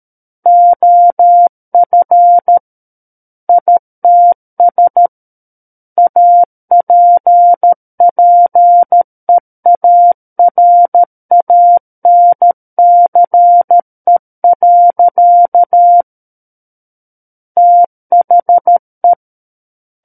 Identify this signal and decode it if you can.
Morse code, heard as a keyed tone, about 13 wpm, 700 Hz. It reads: OF ITS APPEARANCE. THE
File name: Book One / 04-Chapter 4 - 13 WPM